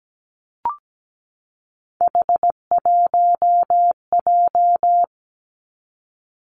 Morse code, heard as a keyed tone, about 17 wpm, 700 Hz. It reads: H1J
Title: Three Challenging Letter-Number Combinations 17wpm